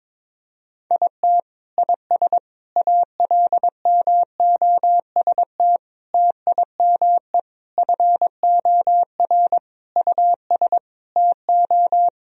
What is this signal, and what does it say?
Morse code, heard as a keyed tone, about 22 wpm, 700 Hz. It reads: IT IS ALMOST TIME FOR US TO
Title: Sentences from Top 1000 Words - Review Part 4 - 22wpm